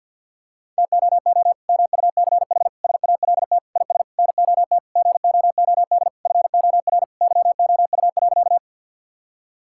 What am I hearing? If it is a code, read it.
Morse code, heard as a keyed tone, about 50 wpm, 700 Hz. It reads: TOO MUCH SALT IS NOT GOOD FOR YOU.
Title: Sentences from Top 1000 Words - Review Part 5 - 50wpm